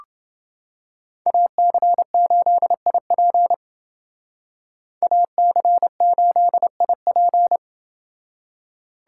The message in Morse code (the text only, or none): AC8IP AC8IP